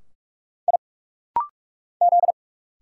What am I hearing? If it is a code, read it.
Morse code, heard as a keyed tone, about 46 wpm, 700 Hz. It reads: I Z
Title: Single Letters - Rapid-Fire 45wpm